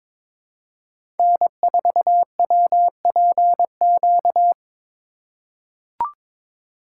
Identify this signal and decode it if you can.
Morse code, heard as a keyed tone, about 22 wpm, 700 Hz. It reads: N4WPQ
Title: USA Call Signs - 5 Characters - 3x Character Spacing - 22wpm